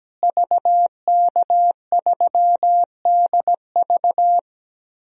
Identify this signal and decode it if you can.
Morse code, heard as a keyed tone, about 17 wpm, 700 Hz. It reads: VK3DV
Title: DXCC-100 Call Signs - 5 Characters - Repeated 2x - 17wpm